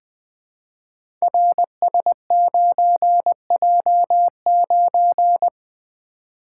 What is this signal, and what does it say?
Morse code, heard as a keyed tone, about 21 wpm, 700 Hz. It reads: RS9J9
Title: Code Groups - 5 Letter:Numbers - 20wpm